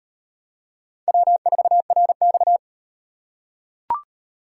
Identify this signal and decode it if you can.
Morse code, heard as a keyed tone, about 38 wpm, 700 Hz. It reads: W4RX